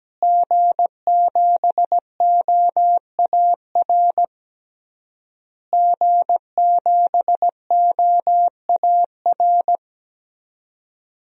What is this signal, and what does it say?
Morse code, heard as a keyed tone, about 17 wpm, 700 Hz. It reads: G7OAR G7OAR
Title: DXCC-100 Call Signs - 5 Characters - Repeated 3x - 17wpm